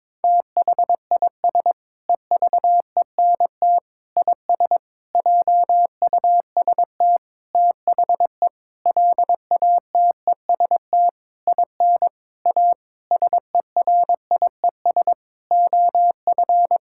Morse code, heard as a keyed tone, about 22 wpm, 700 Hz. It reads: THIS EVENT IS JUST THE LATEST IN A SERIES OF
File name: practice / 22wpm / Sentences with 8-Word Phrases - 22wpm